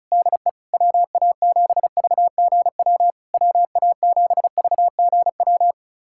Morse code, heard as a keyed tone, about 36 wpm, 700 Hz. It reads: DE WA7VGW WA7VGW
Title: SOTA QSO Practice 35wpm